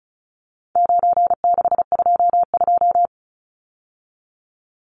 Morse code, heard as a keyed tone, about 34 wpm, 700 Hz. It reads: 9622
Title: Numbers - 4-Digit 35wpm